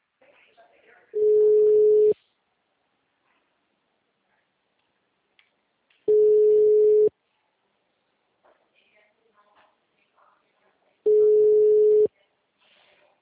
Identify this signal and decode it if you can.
Ringback tone (Europe, 425 Hz), with speech.
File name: out- -2014- - - .0